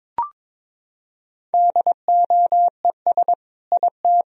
Morse code, heard as a keyed tone, about 22 wpm, 700 Hz. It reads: DOES IT
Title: Sentences from Top 1000 Words - Review Part 2 - 22wpm